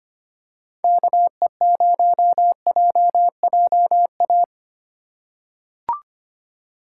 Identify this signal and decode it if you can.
Morse code, heard as a keyed tone, about 25 wpm, 700 Hz. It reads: KE0JJA
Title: USA Call Signs - 6 Characters - 4x Character Spacing - 25wpm